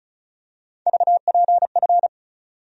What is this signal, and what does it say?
Morse code, heard as a keyed tone, about 34 wpm, 700 Hz. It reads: VPF
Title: USA Call Sign Postfixes - 3 Letters 35wpm